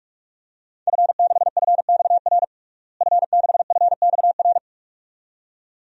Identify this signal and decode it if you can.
Morse code, heard as a keyed tone, about 46 wpm, 700 Hz. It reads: F6FXR F6FXR
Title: DXCC-100 Call Signs - 5 Characters - Repeated 2x - 45wpm